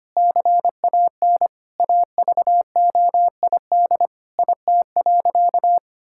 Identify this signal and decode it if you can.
Morse code, heard as a keyed tone, about 25 wpm, 700 Hz. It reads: CAN AVOID IT.
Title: Sentences from Top 1000 Words - Review Part 6 - 25wpm